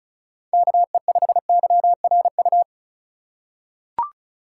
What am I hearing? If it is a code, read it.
Morse code, heard as a keyed tone, about 36 wpm, 700 Hz. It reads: KE5YRU
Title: USA Call Signs - 6 Characters - 4x Character Spacing - 35wpm